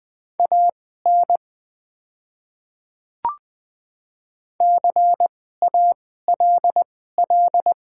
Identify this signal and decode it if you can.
Morse code, heard as a keyed tone, about 20 wpm, spaced as for 14 wpm, 700 Hz. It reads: AN CALL
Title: 2x Character Spacing - Top 100 Words - 20wpm